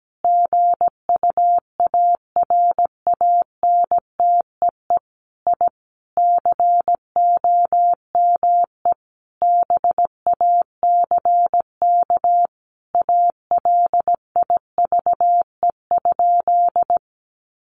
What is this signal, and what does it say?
Morse code, heard as a keyed tone, about 17 wpm, 700 Hz. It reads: GUARANTEE I COME BACK ALIVE?
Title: A Sound Of Thunder 17wpm